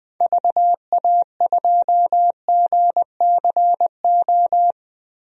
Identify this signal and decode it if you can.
Morse code, heard as a keyed tone, about 21 wpm, 700 Hz. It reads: VA2GCO
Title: DXCC-100 Call Signs - 6 Characters - Repeated 2x - 20wpm